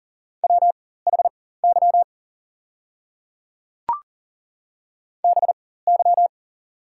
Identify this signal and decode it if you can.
Morse code, heard as a keyed tone, about 40 wpm, spaced as for 19 wpm, 690 Hz. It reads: WHY BY
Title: 4x Character Spacing - Top 500 Words - 40wpm